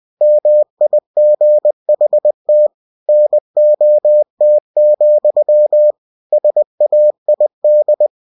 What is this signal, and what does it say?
Morse code, heard as a keyed tone, about 20 wpm, 600 Hz. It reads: MIGHT NOT, SAID
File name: cw-books / winnie-the-pooh / winnie-20wpm-page-0006